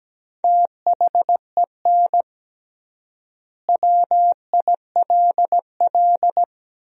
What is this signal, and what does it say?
Morse code, heard as a keyed tone, about 17 wpm, 700 Hz. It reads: THEN WILL
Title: 3x Word-Spacing - Sentences from Top 200 Words - 17wpm